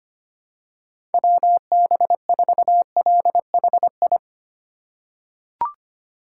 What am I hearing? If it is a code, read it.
Morse code, heard as a keyed tone, about 25 wpm, 700 Hz. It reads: WB4LHI
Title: USA Call Signs - 6 Characters - 6x Character Spacing - 25wpm